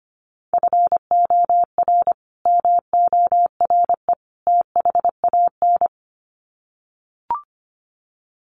Morse code, heard as a keyed tone, about 25 wpm, 700 Hz. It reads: FOR MORE THAN
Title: Sets of 3 Words 25wpm